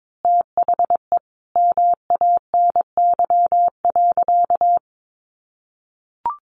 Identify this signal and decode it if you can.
Morse code, heard as a keyed tone, about 22 wpm, 700 Hz. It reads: THE MANY.
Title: Sentences from Top 1300 Words 22wpm